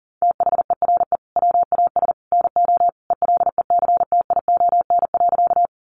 Morse code, heard as a keyed tone, about 40 wpm, 700 Hz. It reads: THERE WAS NO ELECTION.
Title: Sentences from Top 1500 Words 40wpm